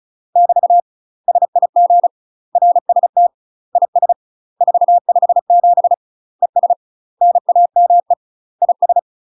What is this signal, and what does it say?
Morse code, heard as a keyed tone, about 34 wpm, 700 Hz. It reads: <BT> SIG RST IS 457 ES NAME IS
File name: UK MASTER QSOs TRAINING / 35WPM / UK MASTER QSOs 0025